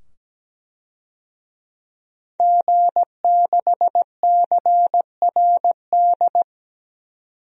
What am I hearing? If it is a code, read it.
Morse code, heard as a keyed tone, about 17 wpm, 700 Hz. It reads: G6CRD
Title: DXCC-100 Call Signs - 5 Characters - 17wpm